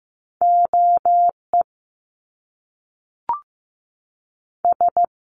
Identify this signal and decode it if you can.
Morse code, heard as a keyed tone, about 15 wpm, spaced as for 3 wpm, 700 Hz. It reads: O ES
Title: Top Two Letter Combinations in Words 15wpm